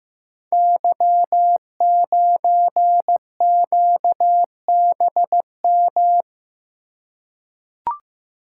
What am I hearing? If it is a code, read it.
Morse code, heard as a keyed tone, about 15 wpm, 700 Hz. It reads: Y9QBM
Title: Code Groups - 5 Letter:Numbers - 15wpm